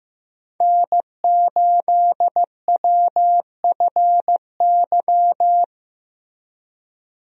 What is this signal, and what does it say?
Morse code, heard as a keyed tone, about 15 wpm, 700 Hz. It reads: N8WFY
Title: USA Call Signs - 5 Characters - 15wpm